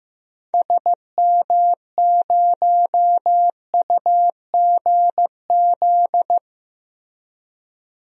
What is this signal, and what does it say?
Morse code, heard as a keyed tone, about 15 wpm, 700 Hz. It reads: SM0UGZ
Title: DXCC-100 Call Signs - 6 Characters - Repeated 3x - 15wpm